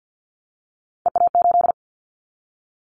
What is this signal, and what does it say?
Morse code, heard as a keyed tone, about 50 wpm, 700 Hz. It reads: EA8
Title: Callsign Prefixes for Top 100 DXCC - 50wpm